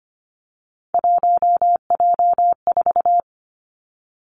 Morse code, heard as a keyed tone, about 25 wpm, 700 Hz. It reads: 1J4
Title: Three Challenging Letter-Number Combinations 25wpm